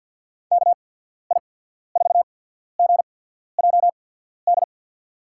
Morse code, heard as a keyed tone, about 52 wpm, spaced as for 14 wpm, 700 Hz. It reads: KI4GJD
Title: USA Call Signs - 6 Characters - 8x Character Spacing - 50wpm